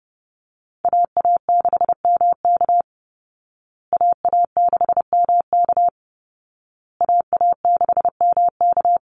Morse code, heard as a keyed tone, about 30 wpm, 700 Hz. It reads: AA6MK AA6MK AA6MK
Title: Call Signs Repeated 2x - 30wpm